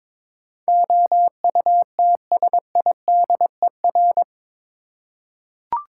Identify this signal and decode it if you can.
Morse code, heard as a keyed tone, about 22 wpm, 700 Hz. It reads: OUTSIDER
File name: practice / 22wpm / Eight Letter Words - 22wpm